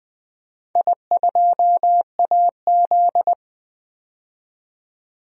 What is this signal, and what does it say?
Morse code, heard as a keyed tone, about 20 wpm, 700 Hz. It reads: I2AZ
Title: DXCC-100 Call Signs - 4 Characters - 20wpm